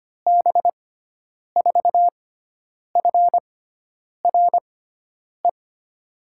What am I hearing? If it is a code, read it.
Morse code, heard as a keyed tone, about 25 wpm, spaced as for 9 wpm, 700 Hz. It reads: B4FRE
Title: USA Call Signs - 6 Characters - 6x Character Spacing - 25wpm